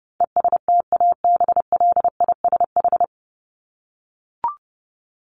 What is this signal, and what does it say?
Morse code, heard as a keyed tone, about 30 wpm, 700 Hz. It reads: ESTABLISH
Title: ED Suffix - Top 100 Words with Base Words 30wpm